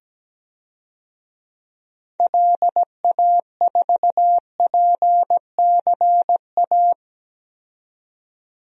Morse code, heard as a keyed tone, about 17 wpm, 700 Hz. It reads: LA4PCA
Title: DXCC-100 Call Signs - 6 Characters - 17wpm